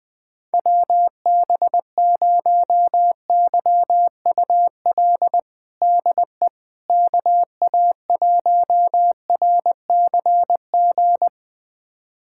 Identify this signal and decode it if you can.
Morse code, heard as a keyed tone, about 20 wpm, 700 Hz. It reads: WB0YUL DE KA1RCG
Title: USA Callsign DE Callsign 20wpm